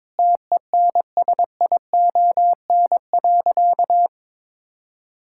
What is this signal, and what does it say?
Morse code, heard as a keyed tone, about 22 wpm, 700 Hz. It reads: TENSION.
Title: Sentences from Top 1600 Words 22wpm